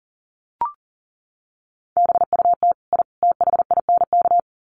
Morse code, heard as a keyed tone, about 40 wpm, 700 Hz. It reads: BUT I THINK
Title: Sets of 3 Words 40wpm